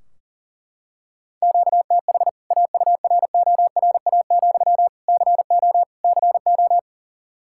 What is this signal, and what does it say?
Morse code, heard as a keyed tone, about 37 wpm, 700 Hz. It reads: QTH AURORA, CO CO
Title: K1USN SST QSO Practice 40wpm